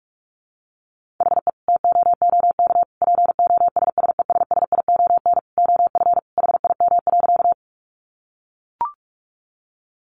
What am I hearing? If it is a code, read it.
Morse code, heard as a keyed tone, about 44 wpm, 700 Hz. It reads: HE TOOK POSSESSION OF HIM.
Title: Sentences from Top 1400 Words 45wpm